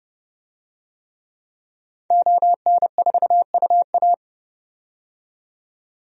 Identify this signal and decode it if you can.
Morse code, heard as a keyed tone, about 30 wpm, 700 Hz. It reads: ON4UA
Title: DXCC-100 Call Signs - 5 Characters - 30wpm